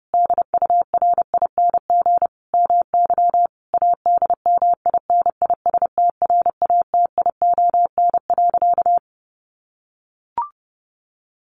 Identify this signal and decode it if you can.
Morse code, heard as a keyed tone, about 30 wpm, 700 Hz. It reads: DURING MY ADMINISTRATION.
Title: Sentences from Top 1300 Words 30wpm